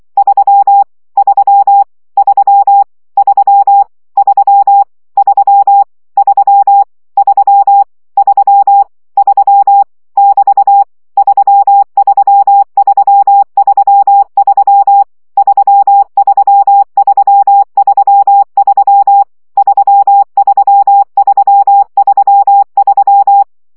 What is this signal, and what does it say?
Morse code, heard as a keyed tone, about 24 wpm, 800 Hz. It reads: 3 3 3 3 3 3 3 3 3 3 <BT> 33333 33333 33333
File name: CW gyakorlas 3 20 baud